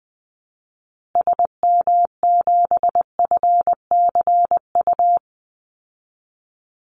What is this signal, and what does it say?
Morse code, heard as a keyed tone, about 20 wpm, 700 Hz. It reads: SM7FCU
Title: Call Signs Repeated 2x - 20wpm